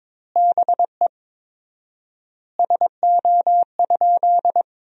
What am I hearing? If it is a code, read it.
Morse code, heard as a keyed tone, about 22 wpm, 700 Hz. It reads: BE SO?
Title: 4x Word-Spacing - Sentences from Top 300 Words - 22wpm